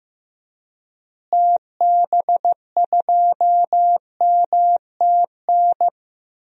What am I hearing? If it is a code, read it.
Morse code, heard as a keyed tone, about 15 wpm, 700 Hz. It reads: TB2MTN
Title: DXCC-100 Call Signs - 6 Characters - Repeated 2x - 15wpm